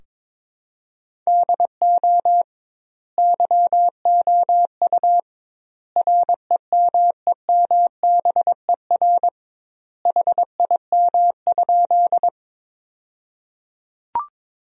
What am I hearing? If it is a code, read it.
Morse code, heard as a keyed tone, about 22 wpm, 700 Hz. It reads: DO YOU REMEMBER HIM?
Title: 2x Word-Spacing - Sentences from Top 400 Words - 22wpm